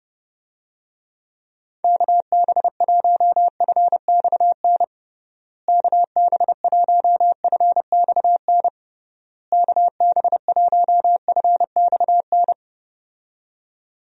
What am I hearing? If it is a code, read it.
Morse code, heard as a keyed tone, about 30 wpm, 700 Hz. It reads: KB1FXN KB1FXN KB1FXN
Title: USA Call Signs - 6 Characters - Repeated 3x - 30wpm